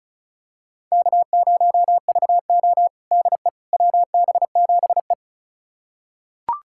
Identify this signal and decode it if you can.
Morse code, heard as a keyed tone, about 36 wpm, 700 Hz. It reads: K0VO DE WB7E
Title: USA Callsign DE Callsign - 4 Characters - 35wpm